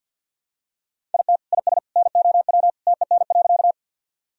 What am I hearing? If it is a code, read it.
Morse code, heard as a keyed tone, about 51 wpm, 700 Hz. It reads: IT IS NOW TEN.
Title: Sentences from Top 1000 Words - Review Part 2 - 50wpm